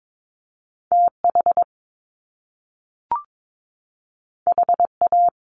Morse code, heard as a keyed tone, about 22 wpm, 700 Hz. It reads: TH HA
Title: Top Two Letter Combinations in Words 22wpm